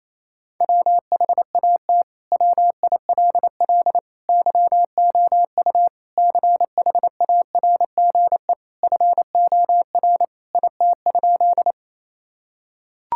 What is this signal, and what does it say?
Morse code, heard as a keyed tone, about 28 wpm, 700 Hz. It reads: WHAT WILL YOU CHARGE FOR IT?
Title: Sentences from Top 1000 Words - Review Part 6 - 28wpm